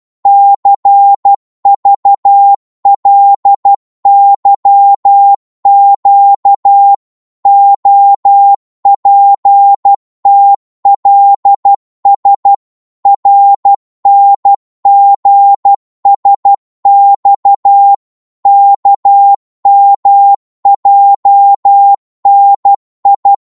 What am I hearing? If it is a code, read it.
Morse code, heard as a keyed tone, about 12 wpm, 800 Hz. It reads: CVLYQ OPTLS RNGSX KMJNI
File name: Morse / ltr-12wpm